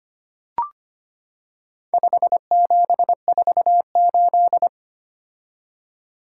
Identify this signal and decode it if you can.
Morse code, heard as a keyed tone, about 25 wpm, 700 Hz. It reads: 5748
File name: practice / 25wpm / Four Letter-Number Combinations - 25wpm